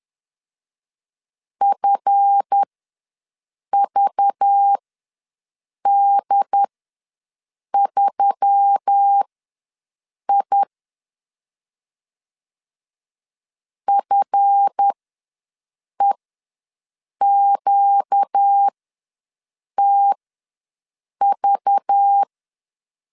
Morse code, heard as a keyed tone, about 10 wpm, spaced as for 6 wpm, 800 Hz. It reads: FVD3I FEQTV
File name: LEZIONI / 7 GF9Z / 7 GF ws 10wpm 3918